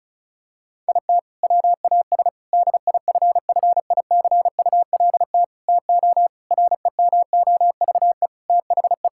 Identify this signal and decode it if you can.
Morse code, heard as a keyed tone, about 36 wpm, 700 Hz. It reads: IT WAS DIFFICULT TO REMOVE THE
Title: Sentences from Top 1000 Words - Review Part 6 - 35wpm